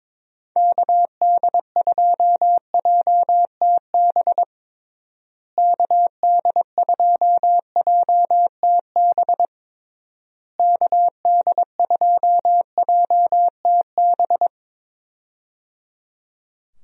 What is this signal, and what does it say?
Morse code, heard as a keyed tone, about 21 wpm, 700 Hz. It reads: KD2JTB KD2JTB KD2JTB
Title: USA Call Signs - 6 Characters - Repeated 3x - 22wpm